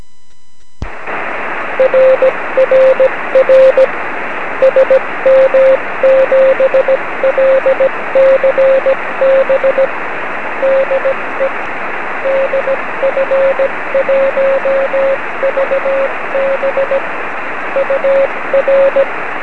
Morse code, heard as a keyed tone, about 17 wpm, 550 Hz, with radio noise. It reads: RRR SM7LCB DE DF1VB UR
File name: 9cm